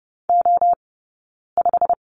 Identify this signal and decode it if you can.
Morse code, heard as a keyed tone, about 30 wpm, 700 Hz. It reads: O 5
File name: Lesson 16d - Letter c - Two Characters 30wpm